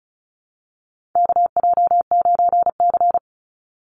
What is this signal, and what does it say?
Morse code, heard as a keyed tone, about 34 wpm, 700 Hz. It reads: KJ9C
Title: Call Signs - 35wpm